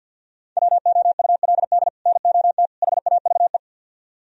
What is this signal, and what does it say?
Morse code, heard as a keyed tone, about 66 wpm, spaced as for 58 wpm, 700 Hz. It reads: WOULD NOT HAVE
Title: Speed-Racing - Sets of 3 Words 50wpm